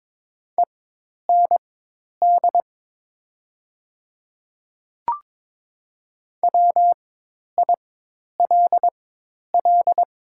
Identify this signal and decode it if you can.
Morse code, heard as a keyed tone, about 22 wpm, spaced as for 10 wpm, 700 Hz. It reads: END WILL
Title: 4x Character Spacing - Top 200 Words - 22wpm